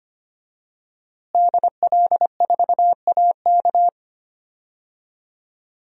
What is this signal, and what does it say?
Morse code, heard as a keyed tone, about 25 wpm, 700 Hz. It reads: DL4AK
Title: DXCC-100 Call Signs - 5 Characters - Repeated 2x - 25wpm